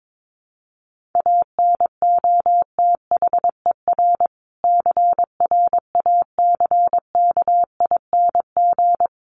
Morse code, heard as a keyed tone, about 22 wpm, 700 Hz. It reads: ANOTHER CRACKING
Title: A Sound Of Thunder 22wpm